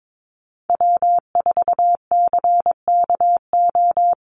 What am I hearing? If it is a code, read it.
Morse code, heard as a keyed tone, about 22 wpm, 700 Hz. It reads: W4CKO
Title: USA Call Signs 22wpm